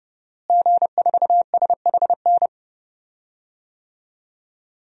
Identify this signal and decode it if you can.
Morse code, heard as a keyed tone, about 32 wpm, 700 Hz. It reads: G4SHN
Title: DXCC-100 Call Signs - 5 Characters - Repeated 2x - 30wpm